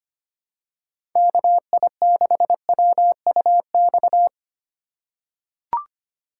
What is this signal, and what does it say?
Morse code, heard as a keyed tone, about 25 wpm, 700 Hz. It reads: KI6WUX
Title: USA Call Signs - 4x Character Spacing - 25wpm